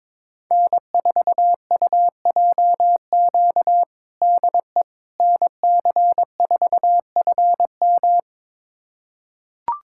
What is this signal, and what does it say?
Morse code, heard as a keyed tone, about 22 wpm, 700 Hz. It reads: N4UJQ DE NC4FM
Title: USA Callsign DE Callsign - 5 Characters - 22wpm